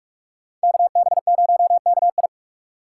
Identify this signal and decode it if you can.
Morse code, heard as a keyed tone, about 45 wpm, 700 Hz. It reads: KB0KI